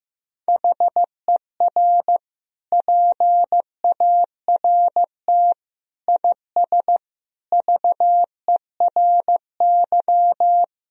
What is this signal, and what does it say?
Morse code, heard as a keyed tone, about 15 wpm, 700 Hz. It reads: HER PART IS VERY
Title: Sentences from Top 1000 Words - Review Part 1 - 15wpm